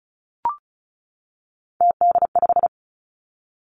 Morse code, heard as a keyed tone, about 36 wpm, 700 Hz. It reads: TD5
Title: Three Challenging Letter-Number Combinations 35wpm